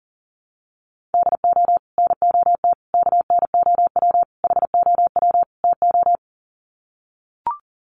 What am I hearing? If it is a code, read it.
Morse code, heard as a keyed tone, about 40 wpm, 700 Hz. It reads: DO NOT KNOW HOW TO
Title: Sets of 5 Words 40wpm